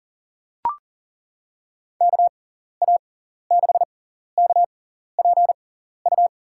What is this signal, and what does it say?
Morse code, heard as a keyed tone, about 37 wpm, spaced as for 14 wpm, 700 Hz. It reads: KA6KPU
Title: USA Call Signs - 6 Characters - 6x Character Spacing - 40wpm